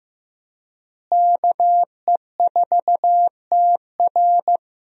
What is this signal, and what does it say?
Morse code, heard as a keyed tone, about 15 wpm, 700 Hz. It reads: KE4TR
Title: USA Call Signs - 5 Characters - Repeated 3x - 15wpm